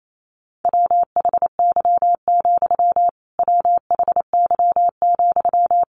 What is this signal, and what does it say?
Morse code, heard as a keyed tone, about 28 wpm, 700 Hz. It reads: WHY, WHY,
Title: A Sound Of Thunder 28wpm